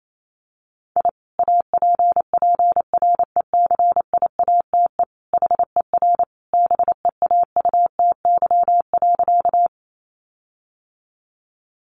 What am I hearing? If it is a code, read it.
Morse code, heard as a keyed tone, about 28 wpm, 700 Hz. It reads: I APPRECIATE HER BEAUTY.
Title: Sentences from Top 1300 Words 28wpm